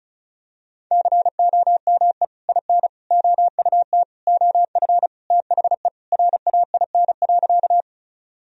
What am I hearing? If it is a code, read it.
Morse code, heard as a keyed tone, about 36 wpm, 700 Hz. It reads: COME IN OUT OF THE RAIN.
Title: Sentences from Top 1000 Words - Review Part 1 - 35wpm